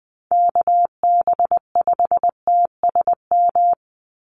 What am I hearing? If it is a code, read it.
Morse code, heard as a keyed tone, about 20 wpm, 700 Hz. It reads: KB5TSM
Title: USA Call Signs 20wpm